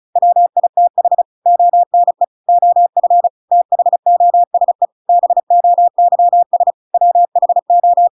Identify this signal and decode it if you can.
Morse code, heard as a keyed tone, about 34 wpm, 700 Hz. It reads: WITH ONE OF THOSE BOYS WHO